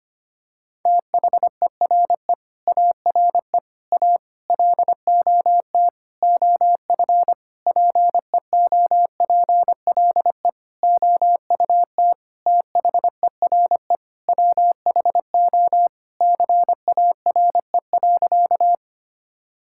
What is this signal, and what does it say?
Morse code, heard as a keyed tone, about 25 wpm, 700 Hz. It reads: THERE ARE A LOT OF PEOPLE OUT THERE WHO CARE.
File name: practice / 25wpm / Sentences with 9-Word Phrases - 25wpm